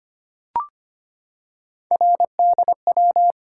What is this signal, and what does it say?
Morse code, heard as a keyed tone, about 25 wpm, 700 Hz. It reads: RDW
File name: Three Letter-Number Combinations - 25wpm